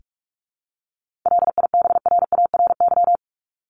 Morse code, heard as a keyed tone, about 44 wpm, 700 Hz. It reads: LIBRARY
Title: Top 501-750 Words 45wpm